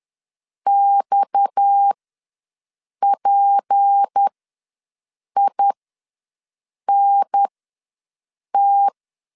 Morse code, heard as a keyed tone, about 10 wpm, spaced as for 6 wpm, 800 Hz. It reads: XPINT
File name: LEZIONI / 6 PKX0 / 6 PKX0 ws 10wpm 4220